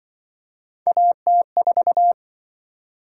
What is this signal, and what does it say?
Morse code, heard as a keyed tone, about 27 wpm, 700 Hz. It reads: AT4